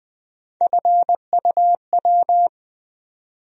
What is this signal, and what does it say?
Morse code, heard as a keyed tone, about 21 wpm, 700 Hz. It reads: FUW
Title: USA Call Sign Postfixes - 3 Letters 20wpm